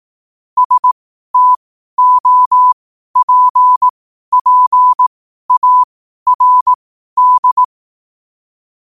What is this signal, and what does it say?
Morse code, heard as a keyed tone, about 18 wpm, spaced as for 12 wpm, 1000 Hz. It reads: STOPPARD
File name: qotdfiles / 12wpm